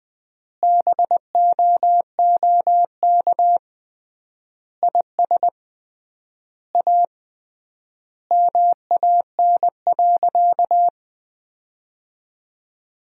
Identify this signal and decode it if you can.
Morse code, heard as a keyed tone, about 20 wpm, 700 Hz. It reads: BOOK IS A MAN.
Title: 3x Word-Spacing - Sentences from Top 500 Words - 20wpm